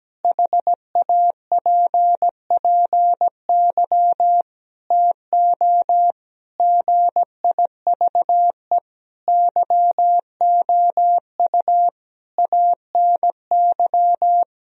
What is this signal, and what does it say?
Morse code, heard as a keyed tone, about 17 wpm, 700 Hz. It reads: HAPPY TO GIVE YOU ANY
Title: Sentences from Top 1000 Words - Review Part 3 - 17wpm